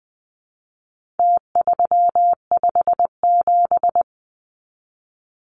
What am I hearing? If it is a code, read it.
Morse code, heard as a keyed tone, about 20 wpm, 700 Hz. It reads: T357
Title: Numbers - Contest Style - 4-Digit 20wpm